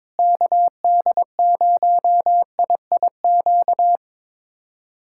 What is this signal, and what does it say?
Morse code, heard as a keyed tone, about 22 wpm, 700 Hz. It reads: KD0IIQ
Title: USA Callsign DE Callsign - 6 Characters - 22wpm